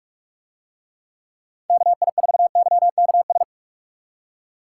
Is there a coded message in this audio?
Yes: Morse code from 1.7 to 3.4 s, heard as a keyed tone, about 41 wpm, 700 Hz. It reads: KI4YKS